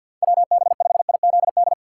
Morse code, heard as a keyed tone, about 47 wpm, 700 Hz. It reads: WB5IZD
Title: USA Call Signs - 4x Character Spacing - 50wpm